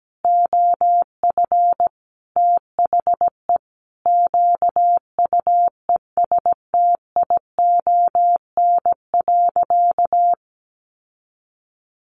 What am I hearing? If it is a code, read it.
Morse code, heard as a keyed tone, about 17 wpm, 700 Hz. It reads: OF THE QUESTION.
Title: Sentences from Top 1500 Words 17wpm